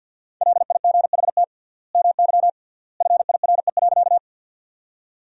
Morse code, heard as a keyed tone, about 50 wpm, 700 Hz. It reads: LIGHT MY FIRE.
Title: 3x Word-Spacing - Sentences from Top 400 Words - 50wpm